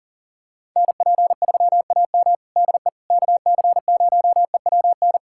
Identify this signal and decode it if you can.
Morse code, heard as a keyed tone, about 40 wpm, 700 Hz. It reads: NP3AM DE KC0EWN
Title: USA Callsign DE Callsign 40wpm